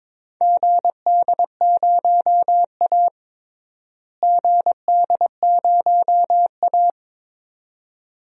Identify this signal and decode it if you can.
Morse code, heard as a keyed tone, about 22 wpm, 700 Hz. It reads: GD0A GD0A
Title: DXCC-100 Call Signs - 4 Characters - Repeated 2x - 22wpm